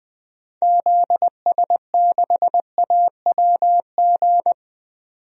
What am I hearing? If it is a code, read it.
Morse code, heard as a keyed tone, about 20 wpm, 700 Hz. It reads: ZS6AWG
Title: DXCC-100 Call Signs - 6 Characters - 20wpm